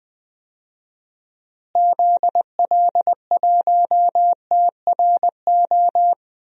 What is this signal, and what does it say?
Morse code, heard as a keyed tone, about 21 wpm, 700 Hz. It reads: ZL1TRO
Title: DXCC-100 Call Signs - 6 Characters - Repeated 2x - 20wpm